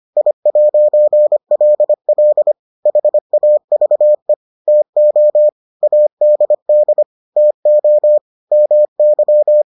Morse code, heard as a keyed tone, about 25 wpm, 600 Hz. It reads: I'LL HAVE TO ADD TO MY